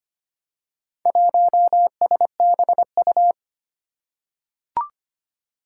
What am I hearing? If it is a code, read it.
Morse code, heard as a keyed tone, about 25 wpm, 700 Hz. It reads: 1SBU
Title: Four Challenging Letter-Number Combinations 25wpm